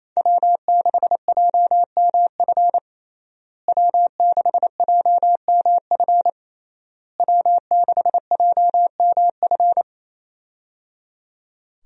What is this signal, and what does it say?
Morse code, heard as a keyed tone, about 27 wpm, 700 Hz. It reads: W6JMF W6JMF W6JMF
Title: USA Call Signs - 5 Characters - Repeated 3x - 28wpm